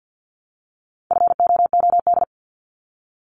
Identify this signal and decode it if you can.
Morse code, heard as a keyed tone, about 50 wpm, 700 Hz. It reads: FOOD
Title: Multi-letter Phonemes Repeated 3x 50wpm